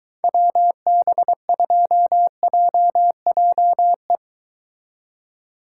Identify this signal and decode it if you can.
Morse code, heard as a keyed tone, about 23 wpm, 700 Hz. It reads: WB2JJE